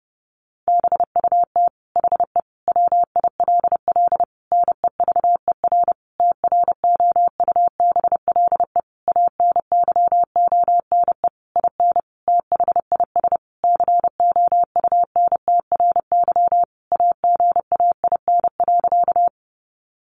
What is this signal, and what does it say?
Morse code, heard as a keyed tone, about 30 wpm, 700 Hz. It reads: BUT HE WILL NEVER TROUBLE ANYONE IN THIS COUNTRY AGAIN.
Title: Sentences from Top 1200 Words 30wpm